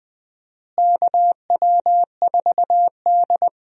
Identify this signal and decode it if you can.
Morse code, heard as a keyed tone, about 20 wpm, 700 Hz. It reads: KW4D
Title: USA Call Signs - 4 Characters - Repeated 3x - 20wpm